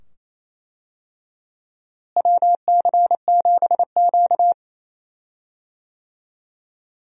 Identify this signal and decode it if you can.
Morse code, heard as a keyed tone, about 28 wpm, 700 Hz. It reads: WC7Q
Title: USA Call Signs - 4 Characters - Repeated 2x - 28wpm